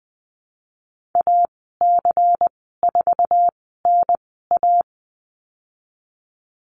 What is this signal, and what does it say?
Morse code, heard as a keyed tone, about 20 wpm, spaced as for 15 wpm, 700 Hz. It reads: AC4NA
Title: 128 Lesson 21g - Number 6 - Call Signs 20wpm